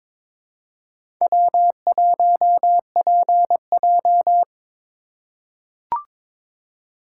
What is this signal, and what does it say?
Morse code, heard as a keyed tone, about 22 wpm, 700 Hz. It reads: W1PJ
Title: USA Call Signs - 4 Characters - 6x Character Spacing - 22wpm